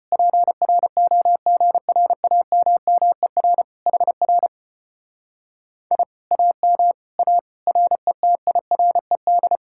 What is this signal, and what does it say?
Morse code, heard as a keyed tone, about 33 wpm, 700 Hz. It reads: PROGRAMMER HR I AM A RETIRED